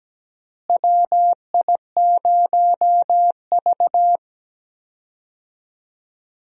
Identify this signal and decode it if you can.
Morse code, heard as a keyed tone, about 17 wpm, 700 Hz. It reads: WI0V
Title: USA Call Signs - 4 Characters - Repeated 2x - 17wpm